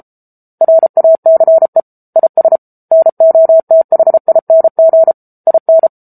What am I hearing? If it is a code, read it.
Morse code, heard as a keyed tone, about 34 wpm, 650 Hz. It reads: RACE IS NOTHING IN
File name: Book Two / 07-Chapter 7 - 33 WPM